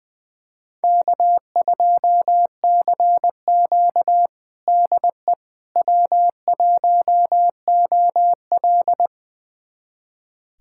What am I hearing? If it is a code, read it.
Morse code, heard as a keyed tone, about 20 wpm, 700 Hz. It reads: K2CQ DE W1OL
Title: USA Callsign DE Callsign - 4 Characters - 20wpm